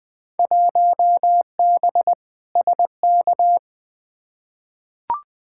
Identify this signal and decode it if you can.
Morse code, heard as a keyed tone, about 20 wpm, 700 Hz. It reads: 1B SK
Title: Field Day Exchanges Encore - 20wpm